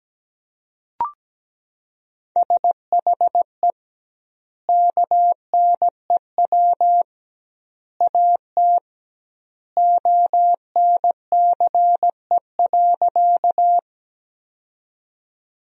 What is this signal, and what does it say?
Morse code, heard as a keyed tone, about 17 wpm, 700 Hz. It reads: SHE KNEW AT ONCE.
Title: 2x Word-Spacing - Sentences from Top 400 Words - 17wpm